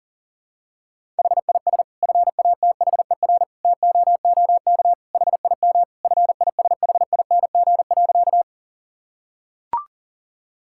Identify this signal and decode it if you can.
Morse code, heard as a keyed tone, about 37 wpm, 700 Hz. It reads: HIS FATHER TOOK HIM FISHING.
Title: Sentences from Top 1000 Words - Review Part 5 - 40wpm